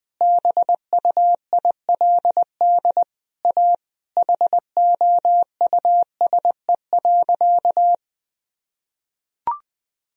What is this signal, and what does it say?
Morse code, heard as a keyed tone, about 20 wpm, 700 Hz. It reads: BUILD A HOUSE.
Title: Sentences from Top 1000 Words - Review Part 4 - 20wpm